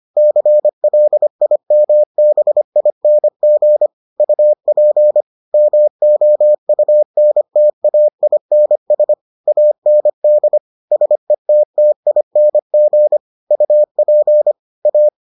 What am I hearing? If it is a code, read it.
Morse code, heard as a keyed tone, about 25 wpm, 600 Hz. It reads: CLIMBING UP MOUNTAINS AND SETTING UP A